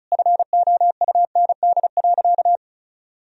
Morse code, heard as a keyed tone, about 36 wpm, 700 Hz. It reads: FOUND.
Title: Sentences from Top 1000 Words - Review Part 4 - 35wpm